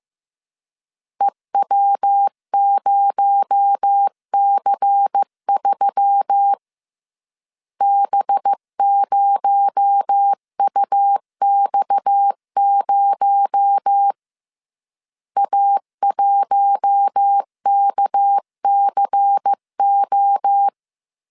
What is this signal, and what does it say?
Morse code, heard as a keyed tone, about 15 wpm, 800 Hz. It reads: EW0C3 B0UX0 A1KCO